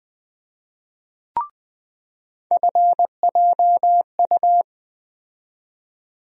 Morse code, heard as a keyed tone, about 20 wpm, 700 Hz. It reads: FJU
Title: USA Call Sign Postfixes - 3 Letters 20wpm